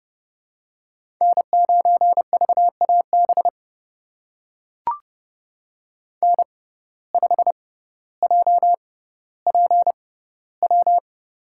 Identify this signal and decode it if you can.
Morse code, heard as a keyed tone, about 31 wpm, spaced as for 16 wpm, 700 Hz. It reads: N9VAB N5JPW
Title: USA Call Signs - 5 Characters - 6x Character Spacing - 30wpm